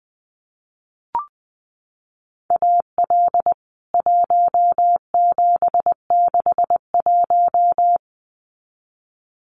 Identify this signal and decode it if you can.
Morse code, heard as a keyed tone, about 20 wpm, 700 Hz. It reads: AL 1761
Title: CWT Exchange 20wpm